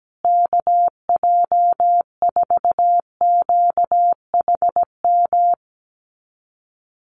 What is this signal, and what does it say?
Morse code, heard as a keyed tone, about 17 wpm, 700 Hz. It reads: KJ4QHM
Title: USA Call Signs Repeated 2x 17wpm